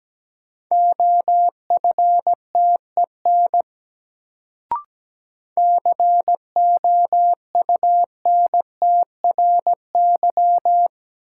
Morse code, heard as a keyed tone, about 17 wpm, 700 Hz. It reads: OFTEN COUNTRY
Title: Sentences from Top 300 Words - New Word Drill - 17wpm